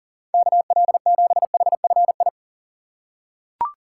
Morse code, heard as a keyed tone, about 37 wpm, 700 Hz. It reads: KL7HFI
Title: K1USN SST QSO Practice - 40wpm